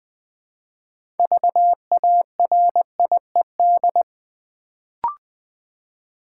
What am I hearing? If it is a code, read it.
Morse code, heard as a keyed tone, about 20 wpm, 700 Hz. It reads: VARIED
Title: Build A Word - 2 Letters At A Time - 6-Letter Words - 20wpm